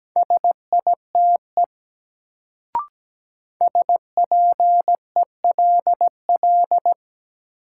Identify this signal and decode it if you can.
Morse code, heard as a keyed tone, about 17 wpm, 700 Hz. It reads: SITE SPELL
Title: Sentences from Top 1000 Words - Review Part 1 - New Word Drill - 17wpm